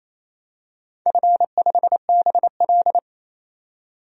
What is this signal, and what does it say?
Morse code, heard as a keyed tone, about 27 wpm, 700 Hz. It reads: F5BL
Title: DXCC-100 Call Signs - 4 Characters - 28wpm